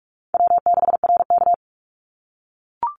Morse code, heard as a keyed tone, about 46 wpm, 700 Hz. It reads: W6RK
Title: Easy Call Signs - 45wpm